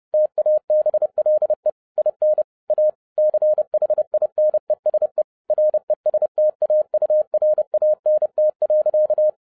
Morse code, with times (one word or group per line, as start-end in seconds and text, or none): 0.1-1.7 TABLE
2.0-2.4 IN
2.7-2.9 A
3.2-5.2 CHINESE
5.5-9.3 RESTAURANT.